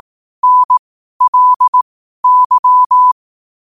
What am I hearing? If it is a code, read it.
Morse code, heard as a keyed tone, about 18 wpm, spaced as for 12 wpm, 1000 Hz. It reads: NLY
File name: qotdfiles / 12wpm